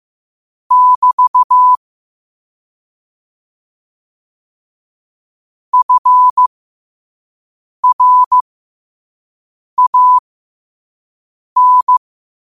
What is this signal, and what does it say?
Morse code, heard as a keyed tone, about 15 wpm, spaced as for 5 wpm, 1000 Hz. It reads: <BT> FRAN
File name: 5wpm